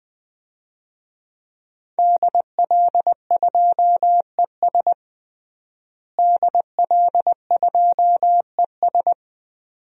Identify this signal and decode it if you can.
Morse code, heard as a keyed tone, about 21 wpm, 700 Hz. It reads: DL2ES DL2ES
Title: DXCC-100 Call Signs - 5 Characters - Repeated 2x - 20wpm